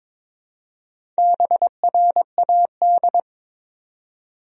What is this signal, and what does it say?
Morse code, heard as a keyed tone, about 22 wpm, 700 Hz. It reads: BRAD
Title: USA Common Names - Max 4 Letters - 22wpm